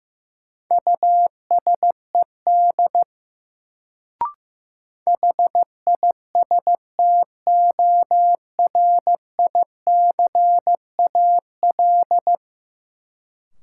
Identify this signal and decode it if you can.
Morse code, heard as a keyed tone, about 15 wpm, 700 Hz. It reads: USED HISTORICAL
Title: Sentences from Top 600 Words - New Word Drill - 15wpm